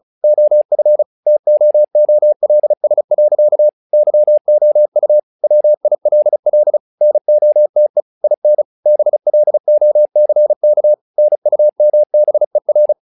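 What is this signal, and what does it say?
Morse code, heard as a keyed tone, about 36 wpm, 600 Hz. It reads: OF TOOLS. YOU WILL NOTE IN BLOCK NUMBER
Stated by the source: Morse practice text "Tower Trouble" sent at 35 WPM, from the SKCC - Straight Key Century Club learning center audio files.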